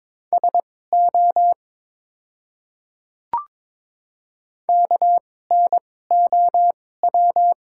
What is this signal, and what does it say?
Morse code, heard as a keyed tone, about 22 wpm, spaced as for 16 wpm, 700 Hz. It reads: SO KNOW
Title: 2x Character Spacing - Top 100 Words - 22wpm